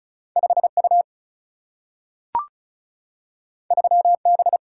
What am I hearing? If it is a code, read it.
Morse code, heard as a keyed tone, about 36 wpm, 700 Hz. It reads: 5U 3B
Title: Single-Letter Single-Number 35wpm